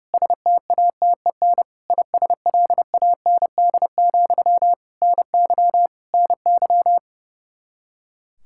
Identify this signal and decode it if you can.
Morse code, heard as a keyed tone, about 30 wpm, 700 Hz. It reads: STATEN ISLAND, NY NY
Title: Practice QSO - Exchange 1 - Beginning - 30wpm